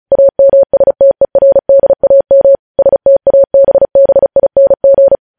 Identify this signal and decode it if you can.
Morse code, heard as a keyed tone, about 36 wpm, 550 Hz. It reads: AMSTERDAM STABBING
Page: News headlines in Morse code at 35 wpm